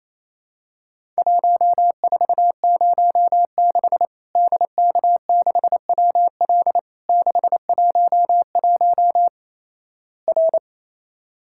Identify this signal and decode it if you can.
Morse code, heard as a keyed tone, about 28 wpm, 700 Hz. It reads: 1406 DK6WL 611 R
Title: Worked All Europe DX Contest - QTC Exchange - 28wpm